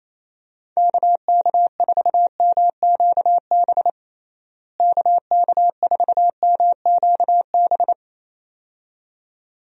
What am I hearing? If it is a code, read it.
Morse code, heard as a keyed tone, about 28 wpm, 700 Hz. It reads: KK4MQB KK4MQB
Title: USA Call Signs - 6 Characters - Repeated 2x - 28wpm